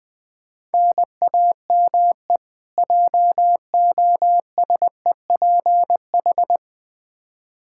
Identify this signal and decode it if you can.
Morse code, heard as a keyed tone, about 20 wpm, 700 Hz. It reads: NAME JOSEPH
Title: Practice QSO - Exchange 1 - Beginning - 20wpm